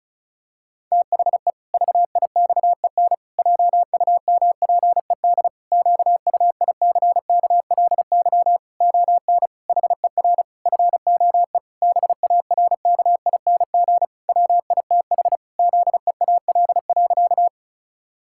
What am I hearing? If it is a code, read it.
Morse code, heard as a keyed tone, about 36 wpm, 700 Hz. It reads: THE VIXEN JUMPED QUICKLY ON HER FOE BARKING WITH ZEAL.
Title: Pangram Challenge v2 - 35wpm